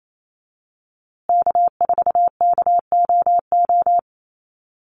Morse code, heard as a keyed tone, about 28 wpm, 700 Hz. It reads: K4KOO
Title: USA Call Signs Repeated 2x 28wpm